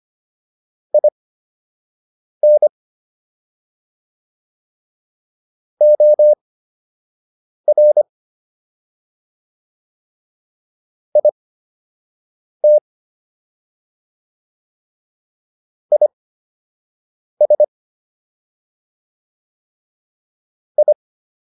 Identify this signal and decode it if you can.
Morse code, heard as a keyed tone, about 25 wpm, spaced as for 6 wpm, 600 Hz. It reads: IN OR IT IS I